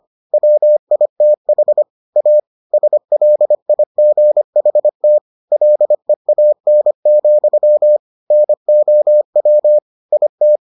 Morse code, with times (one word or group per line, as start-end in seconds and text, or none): 0.3-1.8 WITH
2.2-2.4 A
2.7-5.2 SLIGHT
5.5-8.0 LEAN,
8.3-9.8 NOW
10.1-10.6 IT